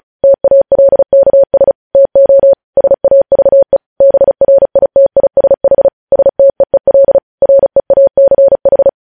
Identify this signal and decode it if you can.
Morse code, heard as a keyed tone, about 36 wpm, 550 Hz. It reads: TALKS TO SAVE BRITISH STEEL REACH
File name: podcast / News headlines-1x-35x35